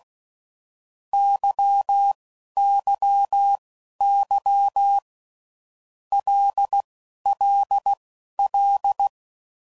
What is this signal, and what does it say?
Morse code, heard as a keyed tone, about 16 wpm, spaced as for 11 wpm, 800 Hz. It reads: YYY LLL
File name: Groups / Basic Letter Groups / LFCYQ